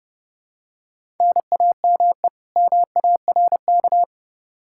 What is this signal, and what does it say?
Morse code, heard as a keyed tone, about 30 wpm, 700 Hz. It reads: NAME MARK
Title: Practice QSO - Exchange 1 - Beginning - 30wpm